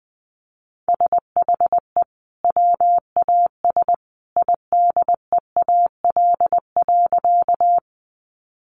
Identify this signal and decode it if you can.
Morse code, heard as a keyed tone, about 20 wpm, 700 Hz. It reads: SHE WAS IDEAL.
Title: Sentences from Top 1400 Words 20wpm